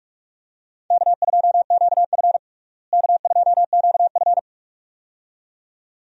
Morse code, heard as a keyed tone, about 45 wpm, 700 Hz. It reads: K2QF K2QF
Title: USA Call Signs - 4 Characters - Repeated 2x - 45wpm